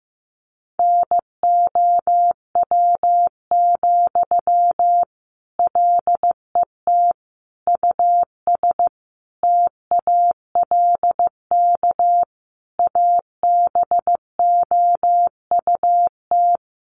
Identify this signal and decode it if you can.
Morse code, heard as a keyed tone, about 15 wpm, 700 Hz. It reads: NOW, LET US TALK ABOUT
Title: Sentences from Top 800 Words 15wpm